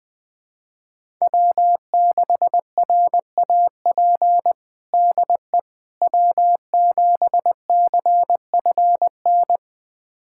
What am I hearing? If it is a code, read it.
Morse code, heard as a keyed tone, about 20 wpm, 700 Hz. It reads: W6RAP DE W7CFN
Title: USA Callsign DE Callsign - 5 Characters - 20wpm